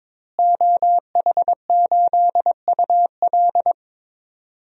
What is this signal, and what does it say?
Morse code, heard as a keyed tone, about 22 wpm, 700 Hz. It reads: OH8UL
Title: DXCC-100 Call Signs - 5 Characters - Repeated 2x - 22wpm